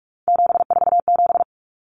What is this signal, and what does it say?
Morse code, heard as a keyed tone, about 44 wpm, 700 Hz. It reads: 747
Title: Numbers - 3-Digit 45wpm